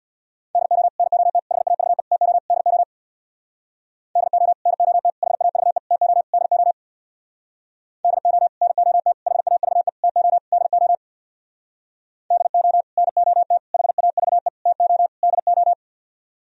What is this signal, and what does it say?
Morse code, heard as a keyed tone, about 63 wpm, 690 Hz. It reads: DO NOT HAVE TO DO DO NOT HAVE TO DO DO NOT HAVE TO DO DO NOT HAVE TO DO